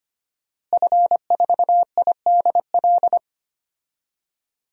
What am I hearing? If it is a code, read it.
Morse code, heard as a keyed tone, about 25 wpm, 700 Hz. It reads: F4IDL
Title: DXCC-100 Call Signs - 5 Characters - Repeated 2x - 25wpm